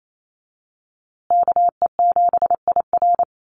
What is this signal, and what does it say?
Morse code, heard as a keyed tone, about 28 wpm, 700 Hz. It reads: KE7IR
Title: USA Call Signs Repeated 2x 28wpm